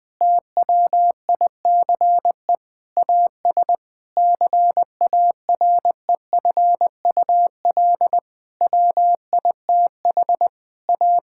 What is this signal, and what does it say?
Morse code, heard as a keyed tone, about 20 wpm, 700 Hz. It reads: TWICE AS CAREFUL WITH A
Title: Sentences from Top 1600 Words 20wpm